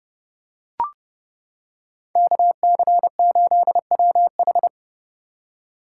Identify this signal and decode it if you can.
Morse code, heard as a keyed tone, about 30 wpm, 700 Hz. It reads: KC8WH
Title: K1USN SST QSO Practice - 30wpm